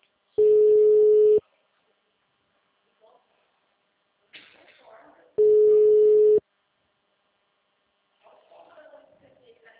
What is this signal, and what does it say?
Ringback tone (Europe, 425 Hz), with speech.
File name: out- -2003- - - .14